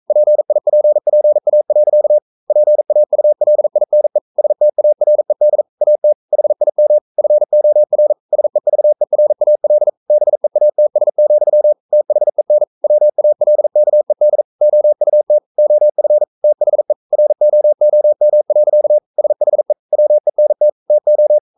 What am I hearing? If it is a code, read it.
Morse code, heard as a keyed tone, about 42 wpm, 600 Hz. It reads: PIPPA. PAULINE STARED AT HIM FOR SEVERAL BEATS, THEN WALKED OUT OF THE ROOM. SHE WENT TO
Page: Romanen Never af Ken Follet på engelsk.